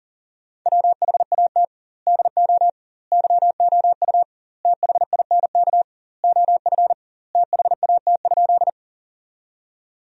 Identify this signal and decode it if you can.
Morse code, heard as a keyed tone, about 40 wpm, 700 Hz. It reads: WHAT DO YOU THINK OF THAT?
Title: 2x Word-Spacing - Sentences from Top 200 Words - 40wpm